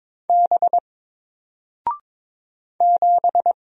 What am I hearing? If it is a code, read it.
Morse code, heard as a keyed tone, about 22 wpm, 700 Hz. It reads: B 7
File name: Single-Letter-Number-22wpm